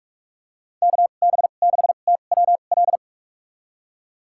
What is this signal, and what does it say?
Morse code, heard as a keyed tone, about 46 wpm, spaced as for 33 wpm, 700 Hz. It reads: KB6TWL
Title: USA Call Signs - 2x Character Spacing - 45wpm